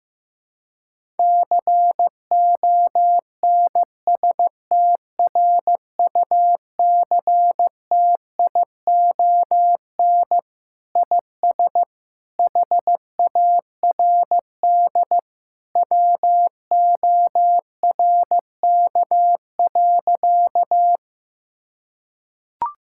Morse code, heard as a keyed tone, about 15 wpm, 700 Hz. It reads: CONSTRUCTION IS HARD WORK.
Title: Sentences from Top 1600 Words 15wpm